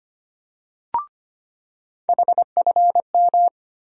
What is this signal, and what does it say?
Morse code, heard as a keyed tone, about 25 wpm, 700 Hz. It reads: HFM
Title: USA Call Sign Postfixes - 3 Letters 25wpm